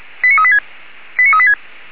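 Five-tone selective call (ZVEI), heard as quick runs of five tones, tones of about 70 ms. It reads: 89386 89386
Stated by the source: Five-tone ZVEI selective call received on an SDR, logged by the fire brigade of Orlamünde (feuerwehr-orlamuende).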